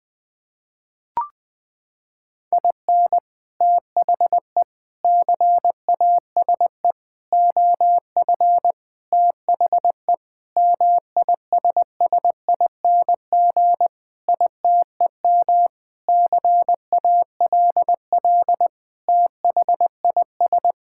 Morse code, heard as a keyed tone, about 20 wpm, 700 Hz. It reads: IN THE CASE OF THE MISSING ITEM CALL THIS
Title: Sentences with 5-Word Phrases - 20wpm